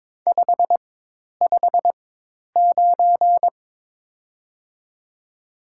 Morse code, heard as a keyed tone, about 22 wpm, 700 Hz. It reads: 5 5 9
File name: Top 90 QSO Elements - 4x Character Spacing - 22wpm